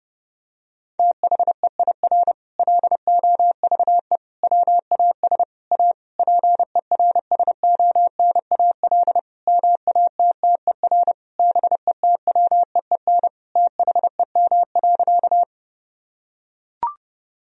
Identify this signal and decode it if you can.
Morse code, heard as a keyed tone, about 30 wpm, 700 Hz. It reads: THEIR LOVE WAS A PERSONAL MATTER BETWEEN THEM.
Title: Sentences from Top 1000 Words - Review Part 4 - 30wpm